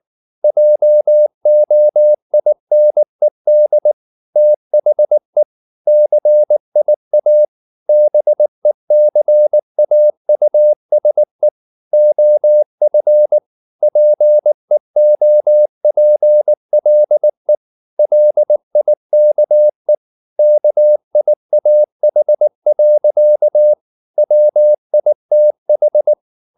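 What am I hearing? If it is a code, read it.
Morse code, heard as a keyed tone, about 19 wpm, 600 Hz. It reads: JOINED THE CIA BECAUSE OF PEOPLE LIKE KIAH. WITH
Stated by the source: Morse code at 19 wpm from CW med Gnister, reading Ken Follett's novel Never in English.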